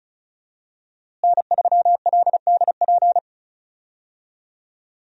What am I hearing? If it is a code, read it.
Morse code, heard as a keyed tone, about 36 wpm, 700 Hz. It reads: N3LDP
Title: USA Call Signs - 5 Characters - Repeated 3x - 35wpm